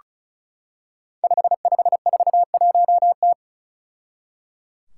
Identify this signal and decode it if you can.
Morse code, heard as a keyed tone, about 36 wpm, 700 Hz. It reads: 5541T
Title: Five Challenging Letter-Number Combinations 35wpm